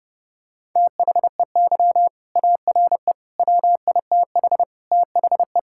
Morse code, heard as a keyed tone, about 32 wpm, 700 Hz. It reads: THEY ARE WITH THE
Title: Sentences from Top 1000 Words - Review Part 5 - 30wpm